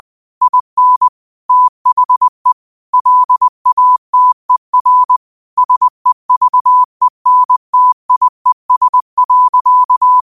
Morse code, heard as a keyed tone, about 20 wpm, 1000 Hz. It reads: IN THE LATER SEVENTIES.